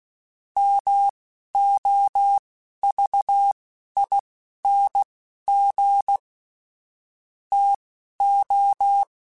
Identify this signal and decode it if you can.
Morse code, heard as a keyed tone, about 16 wpm, spaced as for 12 wpm, 800 Hz. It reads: MOVING TO
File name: QSOs / QSOs-mp3s / WD0BGP 1@15